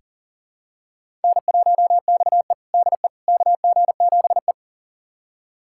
Morse code, heard as a keyed tone, about 40 wpm, 700 Hz. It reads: N1XE DE KG7E
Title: USA Callsign DE Callsign - 4 Characters - 40wpm